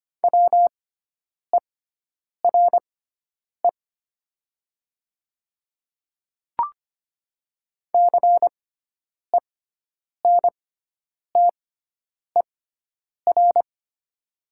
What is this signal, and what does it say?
Morse code, heard as a keyed tone, about 25 wpm, spaced as for 9 wpm, 700 Hz. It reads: WERE CENTER
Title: 6x Character Spacing - Top 500 Words - 25wpm